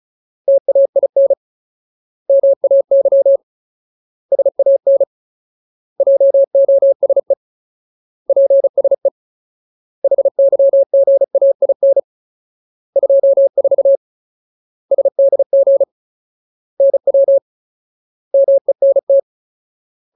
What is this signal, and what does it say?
Morse code, heard as a keyed tone, about 34 wpm, 550 Hz. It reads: TAIN MAY SAN JOSE PSE HYGAIN 24 SDG NW MENT
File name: Ultimate Word List / 35WPM Ultimate